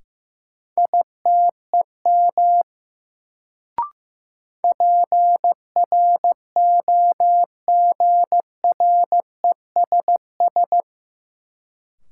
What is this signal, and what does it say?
Morse code, heard as a keyed tone, about 15 wpm, 700 Hz. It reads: ITEM PROGRESS
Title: Sentences from Top 1400 Words - New Word Drill Alt - 15wpm